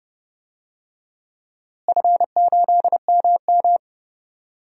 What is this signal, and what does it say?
Morse code, heard as a keyed tone, about 30 wpm, 700 Hz. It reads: F8MM
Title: DXCC-100 Call Signs - 4 Characters - Repeated 3x - 30wpm